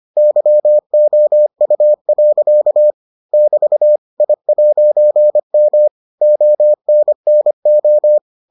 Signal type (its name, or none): Morse code